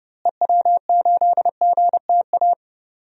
Morse code, heard as a keyed tone, about 31 wpm, 700 Hz. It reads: EW8GTA
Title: DXCC-100 Call Signs - 6 Characters - Repeated 3x - 30wpm